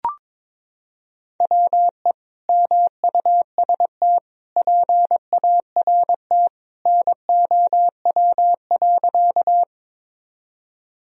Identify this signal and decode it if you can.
Morse code, heard as a keyed tone, about 22 wpm, 700 Hz. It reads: WE MUST PART NOW.
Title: Sentences from Top 1000 Words - Review Part 1 - 22wpm